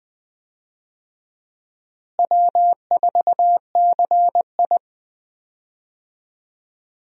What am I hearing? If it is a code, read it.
Morse code, heard as a keyed tone, about 20 wpm, 700 Hz. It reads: W4CI
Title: USA Call Signs - 4 Characters - Repeated 2x - 20wpm